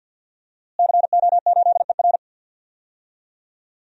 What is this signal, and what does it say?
Morse code, heard as a keyed tone, about 52 wpm, 700 Hz. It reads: CO8ER
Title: DXCC-100 Call Signs - 5 Characters - Repeated 2x - 50wpm